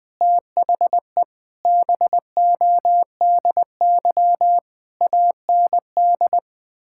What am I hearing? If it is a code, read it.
Morse code, heard as a keyed tone, about 19 wpm, 700 Hz. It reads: THE BODY AND
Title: Difficult Sentences with Hints - 20wpm